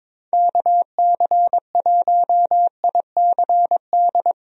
Morse code, heard as a keyed tone, about 22 wpm, 700 Hz. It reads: KC1ICD
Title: USA Call Signs - 6 Characters - Repeated 2x - 22wpm